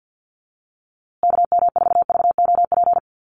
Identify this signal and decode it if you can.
Morse code, heard as a keyed tone, about 50 wpm, 700 Hz. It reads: KM4VOP
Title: USA Call Signs Repeated 2x 50wpm